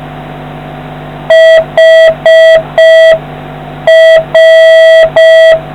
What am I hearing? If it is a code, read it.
Morse code, heard as a keyed tone, about 6 wpm, 650 Hz. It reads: HR